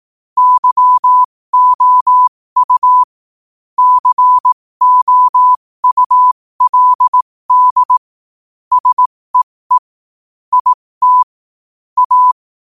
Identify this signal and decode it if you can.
Morse code, heard as a keyed tone, about 18 wpm, spaced as for 15 wpm, 1000 Hz. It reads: YOU COULD SEE IT A
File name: qotdfiles / 15wpm